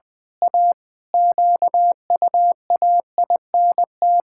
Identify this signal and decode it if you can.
Morse code, heard as a keyed tone, about 20 wpm, 700 Hz. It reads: A QUAINT
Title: Pangram Challenge v2 - 20wpm